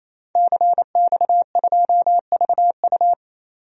Morse code, heard as a keyed tone, about 28 wpm, 700 Hz. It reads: CX2VU
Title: DXCC-100 Call Signs - 5 Characters - 28wpm